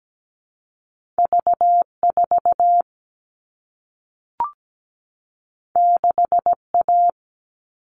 Morse code, heard as a keyed tone, about 17 wpm, 700 Hz. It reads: V4 6A
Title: All Cut Numbers - 2-Digit 17wpm